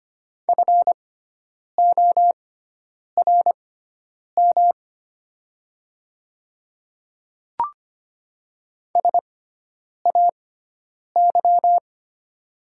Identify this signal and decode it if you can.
Morse code, heard as a keyed tone, about 25 wpm, spaced as for 9 wpm, 700 Hz. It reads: FORM SAY
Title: 6x Character Spacing - Top 200 Words - 25wpm